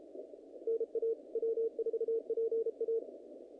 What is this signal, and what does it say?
Morse code, heard as a keyed tone, about 33 wpm, 450 Hz. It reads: NA W4PA
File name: contests / naqp-cw-k5tr-op-k5pi- -2100